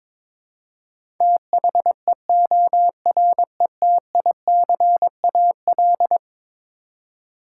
Morse code, heard as a keyed tone, about 22 wpm, 700 Hz. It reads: THEORETICAL
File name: practice / 22wpm / Eleven Letter Words - 22wpm